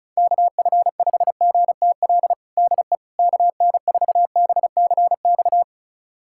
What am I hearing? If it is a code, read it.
Morse code, heard as a keyed tone, about 34 wpm, 700 Hz. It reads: KF5GTL DE KN4BCX
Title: USA Callsign DE Callsign 35wpm